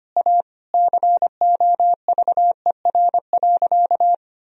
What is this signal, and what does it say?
Morse code, heard as a keyed tone, about 25 wpm, 700 Hz. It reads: A COVER.
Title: Sentences from Top 1000 Words - Review Part 1 - 25wpm